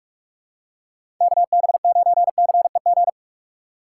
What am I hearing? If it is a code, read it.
Morse code, heard as a keyed tone, about 46 wpm, 700 Hz. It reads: KB9CEG
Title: USA Call Signs - 6 Characters - 2x Character Spacing - 45wpm